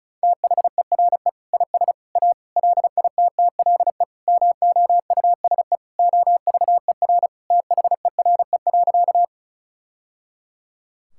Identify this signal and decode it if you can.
Morse code, heard as a keyed tone, about 34 wpm, 700 Hz. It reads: THERE IS A LITTLE MOUSE OVER THERE.
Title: Sentences from Top 1000 Words - Review Part 5 - 35wpm